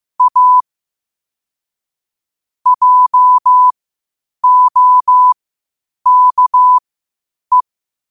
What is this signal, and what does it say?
Morse code, heard as a keyed tone, about 15 wpm, spaced as for 8 wpm, 1000 Hz. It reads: A JOKE
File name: qotdfiles / 8wpm